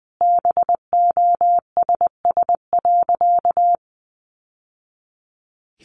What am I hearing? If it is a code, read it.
Morse code, heard as a keyed tone, about 20 wpm, 700 Hz. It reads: BOSS.
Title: Sentences from Top 700 Words 20wpm